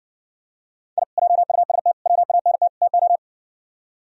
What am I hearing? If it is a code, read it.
Morse code, heard as a keyed tone, about 66 wpm, 700 Hz. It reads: I JUST WANT TO